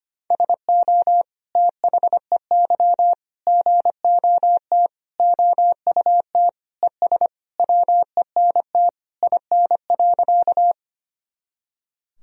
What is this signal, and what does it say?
Morse code, heard as a keyed tone, about 25 wpm, 700 Hz. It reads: SO THEY GOT OUT ES WENT IN.
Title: Sentences from Top 300 Words and Abbreviations 25wpm